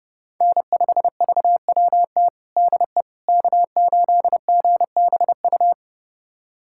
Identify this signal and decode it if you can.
Morse code, heard as a keyed tone, about 32 wpm, 700 Hz. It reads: N5VWT DE K8GBU
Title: USA Callsign DE Callsign - 5 Characters - 30wpm